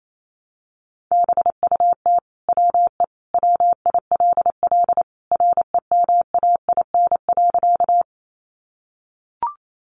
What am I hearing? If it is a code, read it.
Morse code, heard as a keyed tone, about 28 wpm, 700 Hz. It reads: BUT WE WILL REMAIN.
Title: Sentences from Top 1200 Words 28wpm